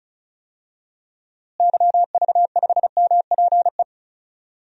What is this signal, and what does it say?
Morse code, heard as a keyed tone, about 34 wpm, 700 Hz. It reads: YV5MPE
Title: DXCC-100 Call Signs - 6 Characters - Repeated 3x - 35wpm